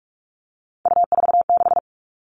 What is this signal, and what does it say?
Morse code, heard as a keyed tone, about 44 wpm, 700 Hz. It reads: U46
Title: All Cut Numbers - 3-Digit 45wpm